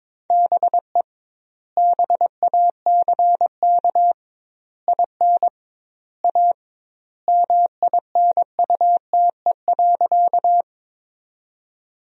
Morse code, heard as a keyed tone, about 21 wpm, 700 Hz. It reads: BE BACK IN A MINUTE.
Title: 2x Word-Spacing - Sentences from Top 500 Words - 22wpm